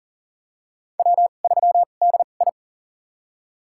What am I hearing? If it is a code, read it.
Morse code, heard as a keyed tone, about 39 wpm, spaced as for 28 wpm, 690 Hz. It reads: W3DI
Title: USA Call Signs - 4 Characters - 2x Character Spacing - 40wpm